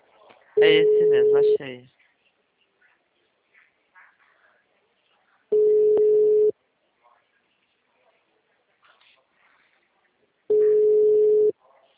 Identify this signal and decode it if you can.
Ringback tone (Europe, 425 Hz), with speech.
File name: out- -2011- - - .6